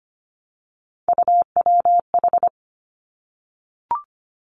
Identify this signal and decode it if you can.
Morse code, heard as a keyed tone, about 25 wpm, 700 Hz. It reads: UWH
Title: Three Challenging Letter-Number Combinations 25wpm